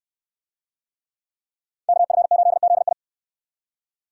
Morse code, heard as a keyed tone, about 60 wpm, 700 Hz. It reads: KF8ZI